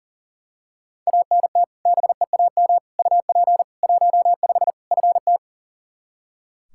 Morse code, heard as a keyed tone, about 40 wpm, 700 Hz. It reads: ANT BEAM UP 15 FT
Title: Practice QSO - Exchange 2 - Info - 40wpm